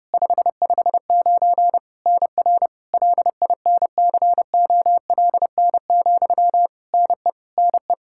Morse code, heard as a keyed tone, about 30 wpm, 700 Hz. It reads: 559 NR LINCOLN, NE NE
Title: Practice QSO - Exchange 1 - Beginning - 30wpm